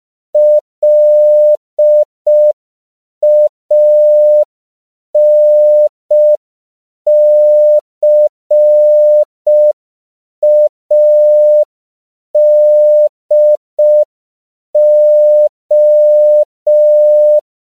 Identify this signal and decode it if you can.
Morse code, heard as a keyed tone, about 5 wpm, 600 Hz. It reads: LANCADO
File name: CW / 5PPM